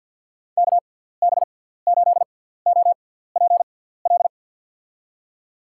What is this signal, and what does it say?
Morse code, heard as a keyed tone, about 47 wpm, spaced as for 17 wpm, 700 Hz. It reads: KB8OPL
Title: USA Call Signs - 6 Characters - 6x Character Spacing - 50wpm